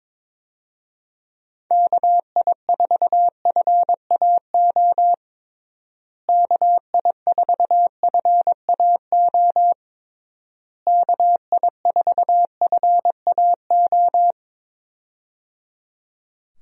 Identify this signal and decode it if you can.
Morse code, heard as a keyed tone, about 22 wpm, 700 Hz. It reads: KI4FAO KI4FAO KI4FAO
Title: USA Call Signs - 6 Characters - Repeated 3x - 22wpm